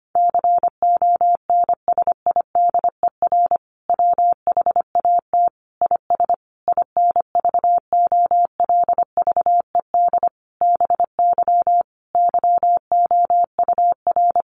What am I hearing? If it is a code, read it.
Morse code, heard as a keyed tone, about 25 wpm, 700 Hz. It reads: CONSIDER WHAT IS INVOLVED BY YOUR
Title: Sentences from Top 800 Words 25wpm